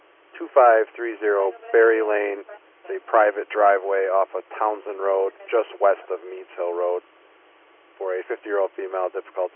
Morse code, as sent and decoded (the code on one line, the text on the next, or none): none